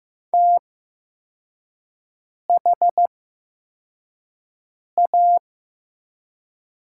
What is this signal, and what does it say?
Morse code, heard as a keyed tone, about 15 wpm, spaced as for 4 wpm, 700 Hz. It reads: THA
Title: 8x Character Spacing - Top 100 Words - 15wpm